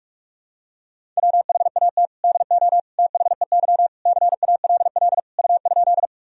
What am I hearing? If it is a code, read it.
Morse code, heard as a keyed tone, about 46 wpm, 700 Hz. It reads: WHAT DO THEY CALL U?
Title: Sentences from Top 100 Words and Abbreviations 45wpm